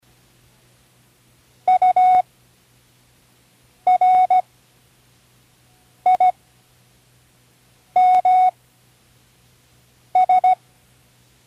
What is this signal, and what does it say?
Morse code, heard as a keyed tone, about 17 wpm, spaced as for 5 wpm, 740 Hz. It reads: URIMS
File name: CWResources / Learning Morse Code (from Tune in the World) / 10 random